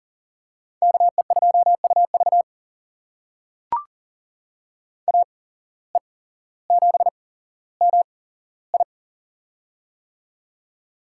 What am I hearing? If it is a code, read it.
Morse code, heard as a keyed tone, about 40 wpm, spaced as for 17 wpm, 700 Hz. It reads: KE2UV AE7MI
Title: USA Call Signs - 5 Characters - 8x Character Spacing - 40wpm